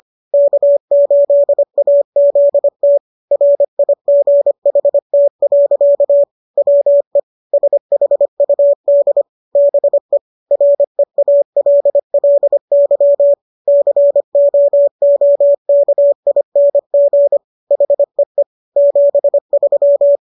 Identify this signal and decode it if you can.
Morse code, heard as a keyed tone, about 25 wpm, 570 Hz. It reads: K8AZT RIGHT. WE SHUD BE REALLY COOKING HEE 73